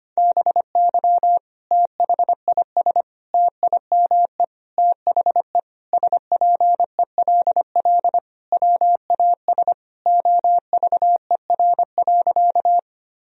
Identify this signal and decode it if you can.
Morse code, heard as a keyed tone, about 25 wpm, 700 Hz. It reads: BY THIS TIME THE SPELL WAS OVER.
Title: Sentences from Top 1000 Words - Review Part 1 - 25wpm